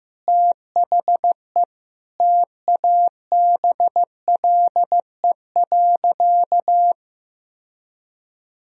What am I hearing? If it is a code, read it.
Morse code, heard as a keyed tone, about 16 wpm, 700 Hz. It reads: THE TABLE.
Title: Sentences from Top 1000 Words - Review Part 2 - 15wpm